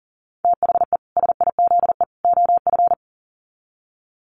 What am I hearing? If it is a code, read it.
Morse code, heard as a keyed tone, about 40 wpm, 700 Hz. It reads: THE SIZE OF
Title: Sets of 3 Words 40wpm